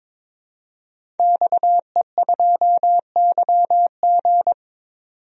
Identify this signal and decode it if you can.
Morse code, heard as a keyed tone, about 22 wpm, 700 Hz. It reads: XE2YG
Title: DXCC-100 Call Signs - 5 Characters - 22wpm